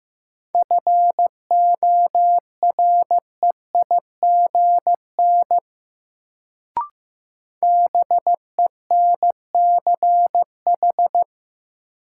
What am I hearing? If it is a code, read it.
Morse code, heard as a keyed tone, about 15 wpm, 700 Hz. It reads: FOREIGN BENCH
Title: Sentences from Top 1400 Words - New Word Drill Alt - 15wpm